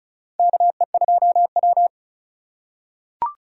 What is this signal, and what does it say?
Morse code, heard as a keyed tone, about 36 wpm, 700 Hz. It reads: KE2W
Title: USA Call Signs - 4 Characters - 4x Character Spacing - 35wpm